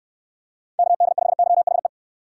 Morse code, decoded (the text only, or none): KD5YHE